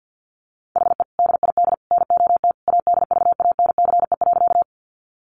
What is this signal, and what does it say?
Morse code, heard as a keyed tone, about 50 wpm, 700 Hz. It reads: HE DID NOT ADVANCE.
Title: Sentences from Top 1400 Words 50wpm